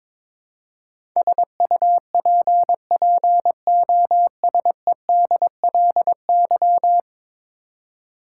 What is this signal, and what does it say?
Morse code, heard as a keyed tone, about 22 wpm, 700 Hz. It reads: SUPPOSEDLY
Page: morse-code-ninja-practice-22wpm